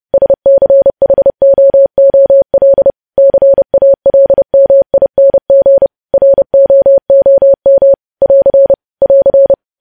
Morse code, heard as a keyed tone, about 30 wpm, 550 Hz. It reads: SCHOOL CALMING ROOM <AR> <AR>
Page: news headlines in morse code at 30 wpm